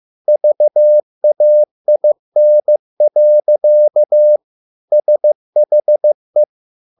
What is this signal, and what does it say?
Morse code, heard as a keyed tone, about 15 wpm, 600 Hz. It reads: VAIN. SHE